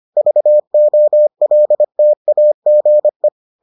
Morse code, heard as a keyed tone, about 25 wpm, 600 Hz. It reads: VOLTAGE